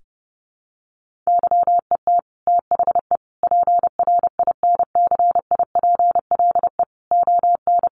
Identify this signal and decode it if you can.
Morse code, heard as a keyed tone, about 30 wpm, 700 Hz. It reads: YET THE PRINCIPLE ON
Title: Sentences from Top 1400 Words 30wpm